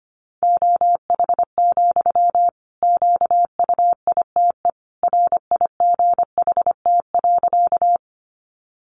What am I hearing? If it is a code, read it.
Morse code, heard as a keyed tone, about 25 wpm, 700 Hz. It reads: OH, QUITE RIGHT.
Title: Sentences from Top 700 Words 25wpm